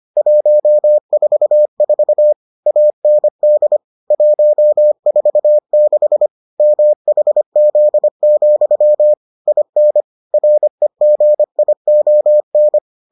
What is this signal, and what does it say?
Morse code, heard as a keyed tone, about 25 wpm, 600 Hz. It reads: 144 AND 146 MHZ, IN REGION